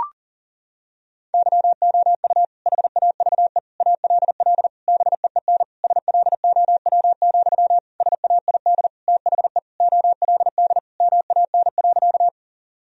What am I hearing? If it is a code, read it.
Morse code, heard as a keyed tone, about 40 wpm, 700 Hz. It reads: YOU HAVE ALL BEEN SLOW, SAID THE OLD MAN.
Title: Sentences from Top 1000 Words - Review Part 2 - 40wpm